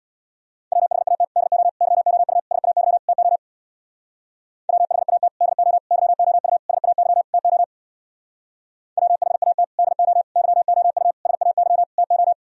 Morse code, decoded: WHAT DO YOU SAY TO WHAT DO YOU SAY TO WHAT DO YOU SAY TO

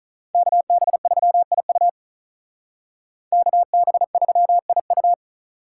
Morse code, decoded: KB3IU KB3IU

